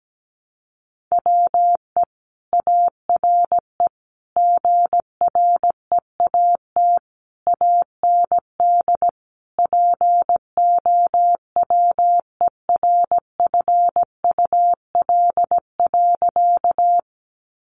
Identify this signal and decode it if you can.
Morse code, heard as a keyed tone, about 17 wpm, 700 Hz. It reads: WE ARE GREAT AND POWERFUL.
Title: Sentences from Top 1300 Words 17wpm